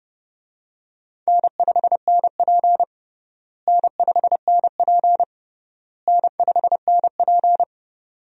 Morse code, heard as a keyed tone, about 29 wpm, 700 Hz. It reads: N5NP N5NP N5NP
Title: USA Call Signs - 4 Characters - Repeated 3x - 30wpm